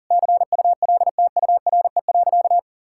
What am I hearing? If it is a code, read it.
Morse code, heard as a keyed tone, about 37 wpm, 690 Hz. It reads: CULTURE.
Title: Sentences from Top 1000 Words - Review Part 6 - 40wpm